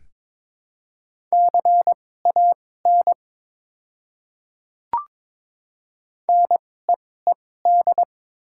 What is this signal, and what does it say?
Morse code, heard as a keyed tone, about 22 wpm, spaced as for 16 wpm, 700 Hz. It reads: CAN NEED
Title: 2x Character Spacing - Top 200 Words - 22wpm